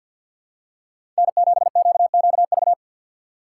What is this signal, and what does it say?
Morse code, heard as a keyed tone, about 49 wpm, 700 Hz. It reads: N7QQV
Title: USA Call Signs - 5 Characters - 3x Character Spacing - 50wpm